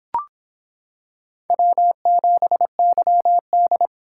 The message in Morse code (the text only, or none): W7YD